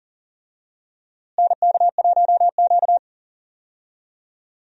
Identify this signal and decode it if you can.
Morse code, heard as a keyed tone, about 40 wpm, 700 Hz. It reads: NK1Q
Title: USA Call Signs - 4 Characters - Repeated 3x - 40wpm